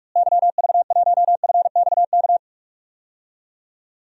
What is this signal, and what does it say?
Morse code, heard as a keyed tone, about 46 wpm, 700 Hz. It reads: YV1FXK
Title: DXCC-100 Call Signs - 6 Characters - Repeated 3x - 45wpm